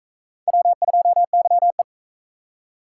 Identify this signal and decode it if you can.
Morse code, heard as a keyed tone, about 36 wpm, 700 Hz. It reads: W2YE